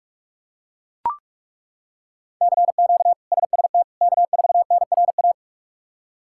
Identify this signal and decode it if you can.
Morse code, heard as a keyed tone, about 41 wpm, 700 Hz. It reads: CQ SST K4NRA
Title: K1USN SST QSO Practice - 45wpm